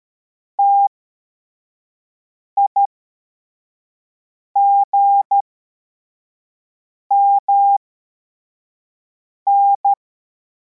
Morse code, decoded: TIGMN